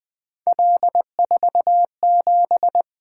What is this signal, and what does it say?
Morse code, heard as a keyed tone, about 20 wpm, 700 Hz. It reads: L47
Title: Three Letter-Number Combinations - 20wpm